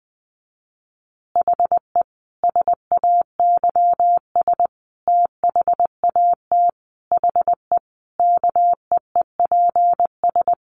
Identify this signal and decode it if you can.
Morse code, heard as a keyed tone, about 19 wpm, 700 Hz. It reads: HE SAYS THAT HE KEEPS
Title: Additional Easy Phrases 20wpm